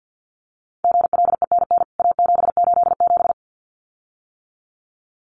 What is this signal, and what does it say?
Morse code, heard as a keyed tone, about 50 wpm, 700 Hz. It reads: GLENN A787
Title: CWT Exchange 50wpm